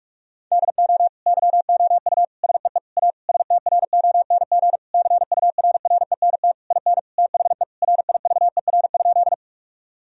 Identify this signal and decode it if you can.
Morse code, heard as a keyed tone, about 46 wpm, 700 Hz. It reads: DO YOU SEE A STRONG CURRENT IN THE RIVER?
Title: Sentences from Top 1000 Words - Review Part 4 - 45wpm